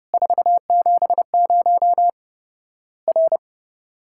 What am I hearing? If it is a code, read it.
Morse code, heard as a keyed tone, about 30 wpm, 700 Hz. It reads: 470 R
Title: Worked All Europe DX Contest - QTC Exchange - 30wpm